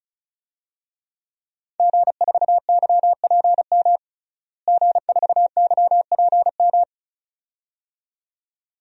Morse code, heard as a keyed tone, about 34 wpm, 700 Hz. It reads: G4YPM G4YPM
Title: DXCC-100 Call Signs - 5 Characters - Repeated 2x - 35wpm